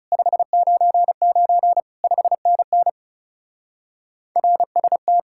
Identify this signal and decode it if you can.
Morse code, heard as a keyed tone, about 36 wpm, 700 Hz. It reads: 599 5NN RST
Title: Speed-Racing Common QSO Phrases - 30wpm